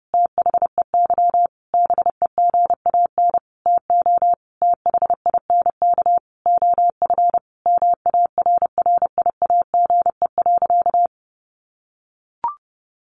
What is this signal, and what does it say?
Morse code, heard as a keyed tone, about 30 wpm, 700 Hz. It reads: THEY BEGAN TO THINK OF MARRIAGE.
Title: Sentences from Top 800 Words 30wpm